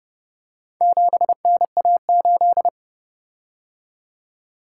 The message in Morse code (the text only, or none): 7NA8